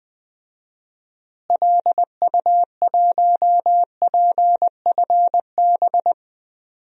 Morse code, heard as a keyed tone, about 20 wpm, 700 Hz. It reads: LU1PFB
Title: DXCC-100 Call Signs - 6 Characters - Repeated 2x - 20wpm